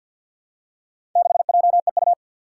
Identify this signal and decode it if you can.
Morse code, heard as a keyed tone, about 47 wpm, 700 Hz. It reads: 6JEU
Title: Four Challenging Letter-Number Combinations 50wpm